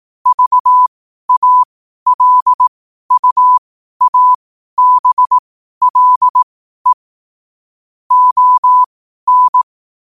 Morse code, heard as a keyed tone, about 18 wpm, spaced as for 12 wpm, 1000 Hz. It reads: VALUABLE ON
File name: qotdfiles / 12wpm